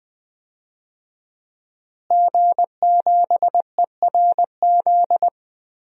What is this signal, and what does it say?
Morse code, heard as a keyed tone, about 21 wpm, 700 Hz. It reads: G7ERZ
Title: DXCC-100 Call Signs - 5 Characters - Repeated 3x - 20wpm